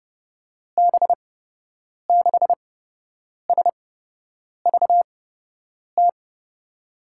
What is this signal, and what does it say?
Morse code, heard as a keyed tone, about 30 wpm, spaced as for 8 wpm, 700 Hz. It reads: B6SVT
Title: USA Call Signs - 8x Character Spacing - 30wpm